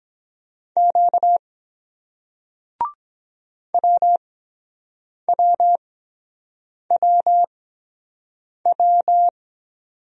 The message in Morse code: Q WWWW

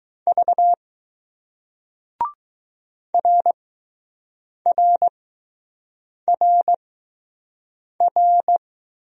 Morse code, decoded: V RRRR

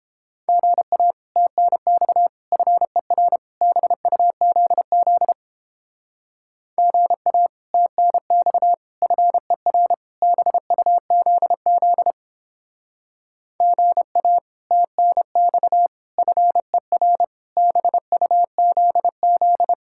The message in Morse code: GA TNX FER BUZZ GA TNX FER BUZZ GA TNX FER BUZZ

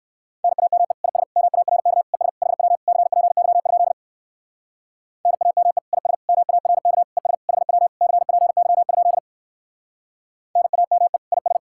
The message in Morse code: NAME IS MARK ES HW CPY? NAME IS MARK ES HW CPY? NAME IS

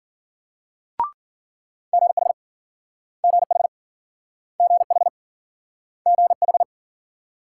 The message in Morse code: GH GH GH GH

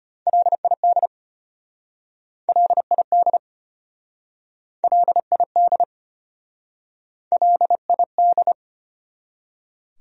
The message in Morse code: LID LID LID LID

